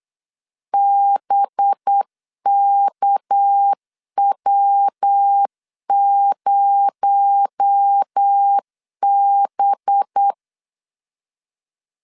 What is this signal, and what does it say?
Morse code, heard as a keyed tone, about 8 wpm, 800 Hz. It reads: BKW0B
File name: LEZIONI / 6 PKX0 / 6 PKX0 8wpm 3341